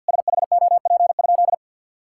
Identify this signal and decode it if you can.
Morse code, heard as a keyed tone, about 50 wpm, 700 Hz. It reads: SHOP?
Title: Sentences from Top 1600 Words 50wpm